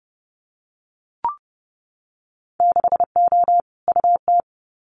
Morse code, heard as a keyed tone, about 30 wpm, 700 Hz. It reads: 6O UT
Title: Winter Field Day Exchanges Encore 30wpm